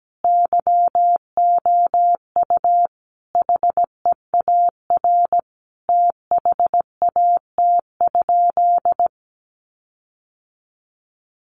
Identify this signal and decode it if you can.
Morse code, heard as a keyed tone, about 17 wpm, 700 Hz. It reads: YOU HEAR THAT?
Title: Sentences from Top 1500 Words 17wpm